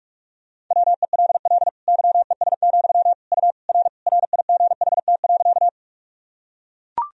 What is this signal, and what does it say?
Morse code, heard as a keyed tone, about 41 wpm, 700 Hz. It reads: WELL YES, U R RIGHT.
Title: Sentences from Top 500 Words and Abbreviations 45wpm